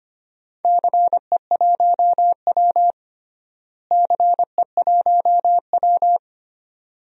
Morse code, heard as a keyed tone, about 25 wpm, 700 Hz. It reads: CE1W CE1W
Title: DXCC-100 Call Signs - 4 Characters - Repeated 2x - 25wpm